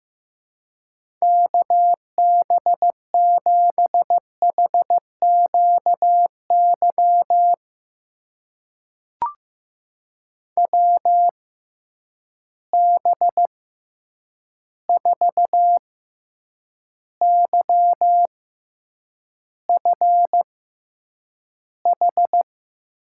Morse code, heard as a keyed tone, about 15 wpm, spaced as for 8 wpm, 700 Hz. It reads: KB7HQY WB4YFH
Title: USA Call Signs - 6 Characters - 6x Character Spacing - 15wpm